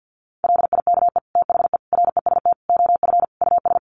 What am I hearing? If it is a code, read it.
Morse code, heard as a keyed tone, about 50 wpm, 700 Hz. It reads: LIKE THE REST OF US
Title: Sets of 5 Words Encore2 50wpm